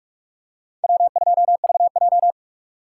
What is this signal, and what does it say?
Morse code, heard as a keyed tone, about 44 wpm, 700 Hz. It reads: W2VJ
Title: USA Call Signs - 4 Characters - 2x Character Spacing - 45wpm